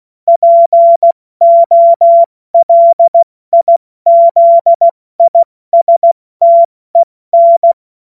Morse code, heard as a keyed tone, about 16 wpm, 680 Hz. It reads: POLIZISTEN
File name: Morsekurs / Witze / Witz 28. Dezember 2021, 70 F80